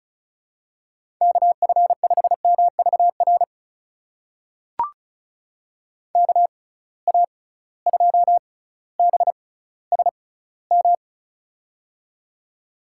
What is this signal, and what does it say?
Morse code, heard as a keyed tone, about 34 wpm, spaced as for 18 wpm, 700 Hz. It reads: KF5MVR KA2BSM
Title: USA Call Signs - 6 Characters - 6x Character Spacing - 35wpm